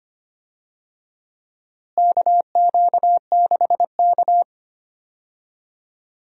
Morse code, heard as a keyed tone, about 25 wpm, 700 Hz. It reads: KQ6K
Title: USA Call Signs - 4 Characters - Repeated 2x - 25wpm